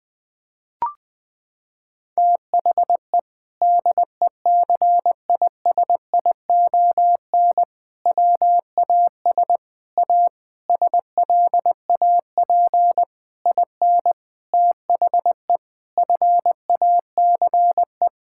Morse code, heard as a keyed tone, about 21 wpm, 700 Hz. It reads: THE DECISION WAS A SLAP IN THE FACE
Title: Sentences with 6-Word Phrases - 20wpm